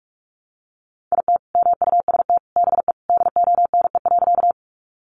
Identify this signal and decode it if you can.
Morse code, heard as a keyed tone, about 44 wpm, 700 Hz. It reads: IT MUST BE DONE.
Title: Sentences from Top 500 Words 45wpm